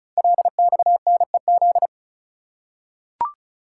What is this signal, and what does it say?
Morse code, heard as a keyed tone, about 36 wpm, 700 Hz. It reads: LXNEZ
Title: Code Groups - 5 Letters - 35wpm